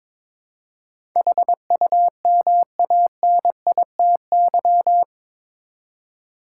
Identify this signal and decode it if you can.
Morse code, heard as a keyed tone, about 22 wpm, 700 Hz. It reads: HUMANITY
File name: practice / 22wpm / Eight Letter Words - 22wpm